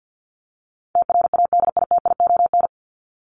Morse code, heard as a keyed tone, about 54 wpm, 700 Hz. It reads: TRADITION
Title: Top 1251-1500 Words 50wpm